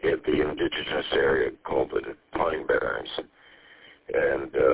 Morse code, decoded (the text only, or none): none